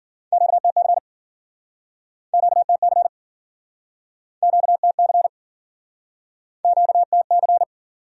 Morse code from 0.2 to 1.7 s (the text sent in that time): QTC